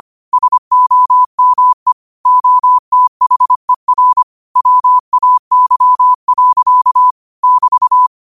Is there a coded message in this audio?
Yes: Morse code from 0.3 to 8.1 s, heard as a keyed tone, about 25 wpm, 1000 Hz. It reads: SOME OTHER WAY. <BT>